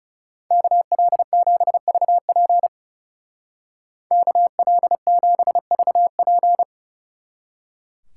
Morse code, heard as a keyed tone, about 36 wpm, 700 Hz. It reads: KL7VP KL7VP
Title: Speed Racing - USA 2x2 Call Signs 30wpm